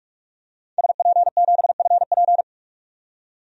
Morse code, heard as a keyed tone, about 46 wpm, 700 Hz. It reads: SP7FP
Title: DXCC-100 Call Signs - 5 Characters - 45wpm